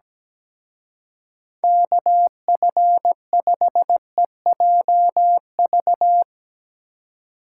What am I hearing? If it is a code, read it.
Morse code, heard as a keyed tone, about 18 wpm, 700 Hz. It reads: KF5EJV
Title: Speed Racing - USA 2x3 Call Signs 17wpm